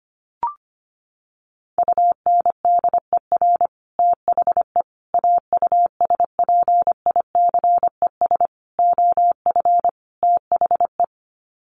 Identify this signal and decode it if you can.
Morse code, heard as a keyed tone, about 25 wpm, 700 Hz. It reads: UNDER THE AUSPICES OF THE
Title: Sets of 5 Words Encore2 25wpm